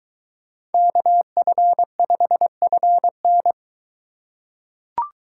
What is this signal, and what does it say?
Morse code, heard as a keyed tone, about 21 wpm, 700 Hz. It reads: KF5FN